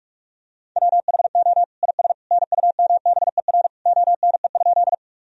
Morse code, heard as a keyed tone, about 43 wpm, 700 Hz. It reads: WHO IS NUMBER ONE?
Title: Sentences from Top 1000 Words - Review Part 1 - 45wpm